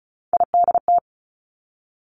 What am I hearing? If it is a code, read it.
Morse code, heard as a keyed tone, about 36 wpm, 700 Hz. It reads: IDT
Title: Three Challenging Letter-Number Combinations 35wpm